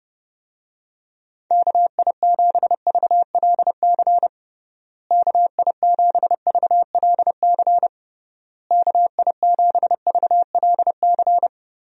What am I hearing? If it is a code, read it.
Morse code, heard as a keyed tone, about 32 wpm, 700 Hz. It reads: KI7VLC KI7VLC KI7VLC
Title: USA Call Signs - 6 Characters - Repeated 3x - 30wpm